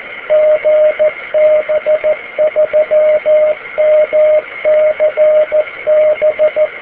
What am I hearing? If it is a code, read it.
Morse code, heard as a keyed tone, about 14 wpm, 610 Hz, with radio noise. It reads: GB3MCB